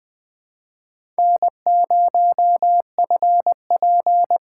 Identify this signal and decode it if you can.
Morse code, heard as a keyed tone, about 20 wpm, 700 Hz. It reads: N0FP
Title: USA Call Signs - 4 Characters - 3x Character Spacing - 20wpm